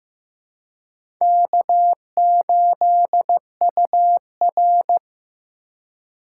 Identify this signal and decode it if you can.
Morse code, heard as a keyed tone, about 16 wpm, 700 Hz. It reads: K8UR
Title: USA Call Signs - 4 Characters - 4x Character Spacing - 15wpm